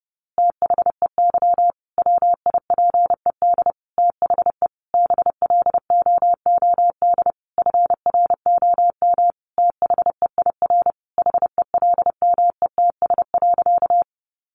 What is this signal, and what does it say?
Morse code, heard as a keyed tone, about 30 wpm, 700 Hz. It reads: THEY WIPED THE BLOOD FROM THEIR HELMETS.
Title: A Sound Of Thunder 30wpm